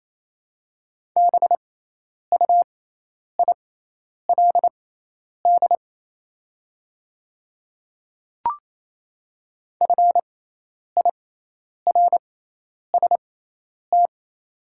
Morse code, heard as a keyed tone, about 28 wpm, spaced as for 10 wpm, 700 Hz. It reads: BUILD FIRST
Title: 6x Character Spacing - Top 200 Words - 28wpm